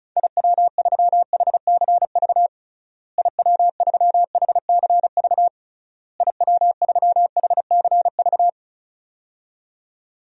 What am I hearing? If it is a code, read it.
Morse code, heard as a keyed tone, about 36 wpm, 700 Hz. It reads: IW3HCV IW3HCV IW3HCV
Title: DXCC-100 Call Signs - 6 Characters - Repeated 3x - 35wpm